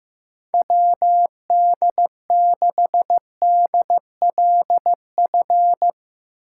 Morse code, heard as a keyed tone, about 15 wpm, 700 Hz. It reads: WD6DLF
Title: USA Call Signs - 6 Characters - 3x Character Spacing - 15wpm